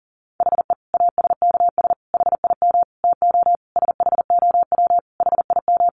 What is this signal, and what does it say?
Morse code, heard as a keyed tone, about 40 wpm, 700 Hz. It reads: HE ASKS HIM TO SHOW HIM
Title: Easy Phrases 40wpm